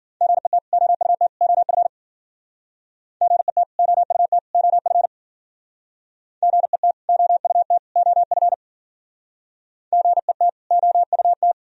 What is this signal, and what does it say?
Morse code, heard as a keyed tone, about 53 wpm, 700 Hz. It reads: GET OUT OF GET OUT OF GET OUT OF GET OUT